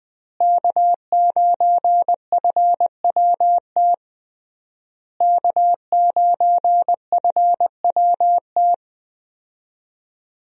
Morse code, heard as a keyed tone, about 21 wpm, 700 Hz. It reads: K9FWT K9FWT
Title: USA Call Signs - 5 Characters - Repeated 3x - 20wpm